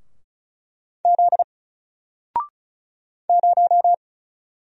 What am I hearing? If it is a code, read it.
Morse code, heard as a keyed tone, about 35 wpm, 700 Hz. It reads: Z 0
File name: Single-Letter-Number-35wpm